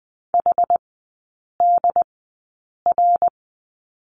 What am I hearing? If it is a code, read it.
Morse code, heard as a keyed tone, about 20 wpm, spaced as for 8 wpm, 700 Hz. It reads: HDR
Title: 093 Lesson 16e - Letter c - Three Characters 20wpm